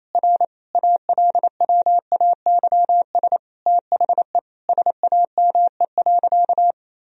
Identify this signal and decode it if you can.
Morse code, heard as a keyed tone, about 27 wpm, 700 Hz. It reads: R ALWAYS THE SAME.
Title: Sentences from Top 300 Words and Abbreviations 28wpm